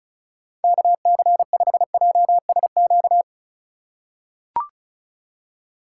Morse code, heard as a keyed tone, about 34 wpm, 700 Hz. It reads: KC5JSQ
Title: USA Call Signs - 6 Characters - 4x Character Spacing - 35wpm